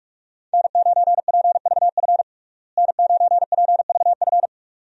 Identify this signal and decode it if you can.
Morse code, heard as a keyed tone, about 47 wpm, 700 Hz. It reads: N9PVF N9PVF
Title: USA Call Signs - 5 Characters - Repeated 2x - 45wpm